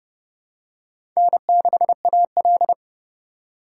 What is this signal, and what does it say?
Morse code, heard as a keyed tone, about 30 wpm, 700 Hz. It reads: N6AL
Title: USA Call Signs - 4 Characters - 2x Character Spacing - 30wpm